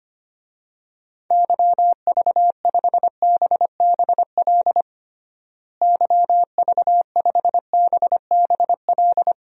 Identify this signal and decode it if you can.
Morse code, heard as a keyed tone, about 25 wpm, 700 Hz. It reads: YV5BBL YV5BBL
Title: DXCC-100 Call Signs - 6 Characters - Repeated 3x - 25wpm